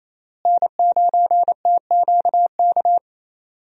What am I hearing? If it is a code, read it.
Morse code, heard as a keyed tone, about 28 wpm, 700 Hz. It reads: N9TQK
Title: USA Call Signs - 5 Characters - Repeated 3x - 28wpm